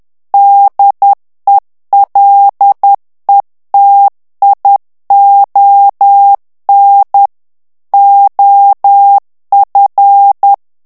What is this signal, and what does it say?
Morse code, heard as a keyed tone, about 10 wpm, 800 Hz. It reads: DELETION OF
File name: ANGLAIS texte clair 3 vitesse 10 mots minute